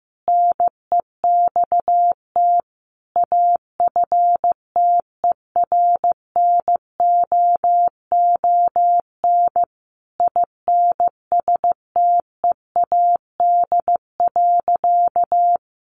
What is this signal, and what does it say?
Morse code, heard as a keyed tone, about 14 wpm, 700 Hz. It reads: NEXT AFTERNOON INSTEAD.
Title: Sentences from Top 1500 Words 15wpm